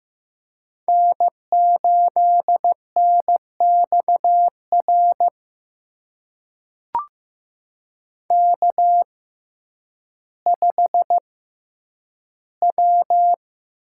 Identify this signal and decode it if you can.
Morse code, heard as a keyed tone, about 15 wpm, spaced as for 9 wpm, 700 Hz. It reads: N8NXR K5W
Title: USA Call Signs - 5 Characters - 6x Character Spacing - 15wpm